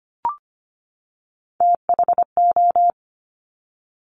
Morse code, heard as a keyed tone, about 25 wpm, 700 Hz. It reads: THO
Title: Top Three Letter Combinations in Words 25wpm